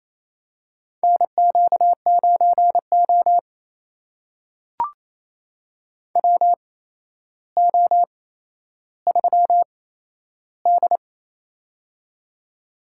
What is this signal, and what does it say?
Morse code, heard as a keyed tone, about 28 wpm, spaced as for 12 wpm, 700 Hz. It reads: NQ9O WO3D
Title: USA Call Signs - 4 Characters - 8x Character Spacing - 28wpm